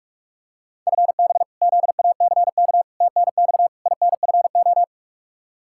Morse code, heard as a keyed tone, about 46 wpm, 700 Hz. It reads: FB ZACK TNX INFO
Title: Practice QSO - Exchange 3 - More Info - 45wpm